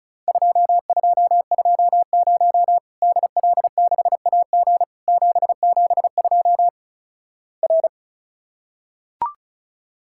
Morse code, heard as a keyed tone, about 36 wpm, 700 Hz. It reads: 2220 DL6AG 772 R
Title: Worked All Europe DX Contest - QTC Exchange - 35wpm